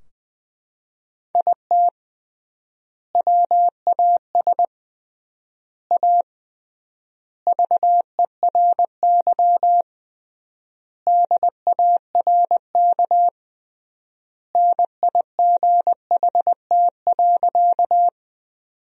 Morse code, heard as a keyed tone, about 20 wpm, 700 Hz. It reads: IT WAS A VERY DARK NIGHT.
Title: 3x Word-Spacing - Sentences from Top 500 Words - 20wpm